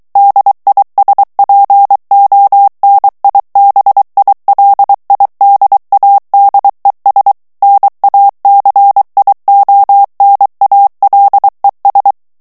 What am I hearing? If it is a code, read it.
Morse code, heard as a keyed tone, about 23 wpm, 800 Hz. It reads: DISPONIBILIDADES NACIONALES
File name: morse mp3 / ESPAGNOL TEXTE EN CLAIR 2 VITESSE 22 MOTS MINUTES